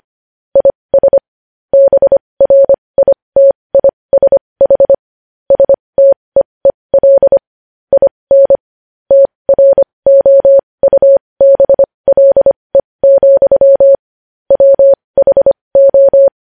Morse code, heard as a keyed tone, about 25 wpm, spaced as for 20 wpm, 550 Hz. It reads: IS BRITISH STEEL IN TROUBLE, WHO
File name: podcast / News headlines-1x-25x20